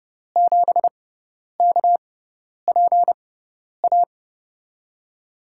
Morse code, heard as a keyed tone, about 30 wpm, spaced as for 10 wpm, 700 Hz. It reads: 7KPA
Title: USA Call Signs - 6x Character Spacing - 30wpm